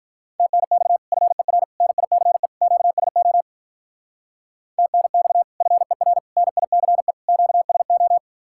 Morse code, heard as a keyed tone, about 43 wpm, 700 Hz. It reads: TNX FER NICE QSO TNX FER NICE QSO
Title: Speed-Racing Common QSO Phrases Encore - 35wpm